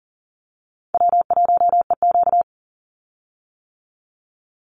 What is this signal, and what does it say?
Morse code, heard as a keyed tone, about 40 wpm, 700 Hz. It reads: W1EQ
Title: Call Signs - 40wpm